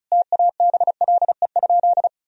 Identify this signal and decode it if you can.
Morse code, heard as a keyed tone, about 36 wpm, 700 Hz. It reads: TABLE?
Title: Sentences from Top 1000 Words - Review Part 4 - 35wpm